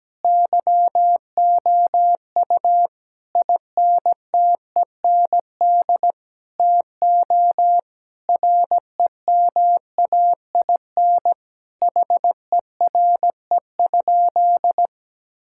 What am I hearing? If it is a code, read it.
Morse code, heard as a keyed tone, about 17 wpm, 700 Hz. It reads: YOU INTEND TO REMAIN HERE?
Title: Sentences from Top 1600 Words 17wpm